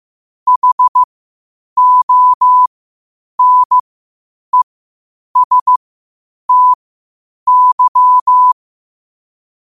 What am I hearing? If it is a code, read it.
Morse code, heard as a keyed tone, about 15 wpm, spaced as for 8 wpm, 1000 Hz. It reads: HONESTY
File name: qotdfiles / 8wpm